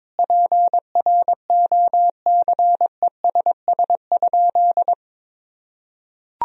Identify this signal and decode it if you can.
Morse code, heard as a keyed tone, about 21 wpm, 700 Hz. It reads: PROCESS?
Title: Sentences from Top 1000 Words - Review Part 3 - 22wpm